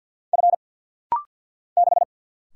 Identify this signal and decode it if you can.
Morse code, heard as a keyed tone, about 50 wpm, 700 Hz. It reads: F 6
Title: Single Letter-Number - Rapid-Fire 50wpm